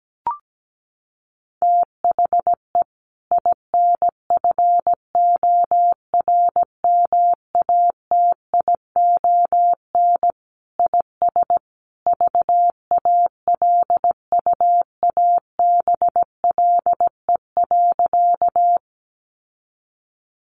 Morse code, heard as a keyed tone, about 17 wpm, 700 Hz. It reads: THE INFORMATION IS VALUABLE.
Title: Sentences from Top 1100 Words 17wpm